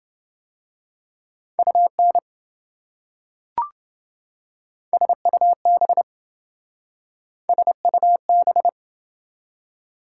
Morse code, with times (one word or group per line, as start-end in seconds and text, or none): 1.6-2.2 UN
4.9-6.0 SUB
7.5-8.7 SUB